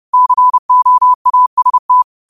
Morse code, heard as a keyed tone, about 30 wpm, 1000 Hz. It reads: COAST
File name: qotdfiles / 30wpm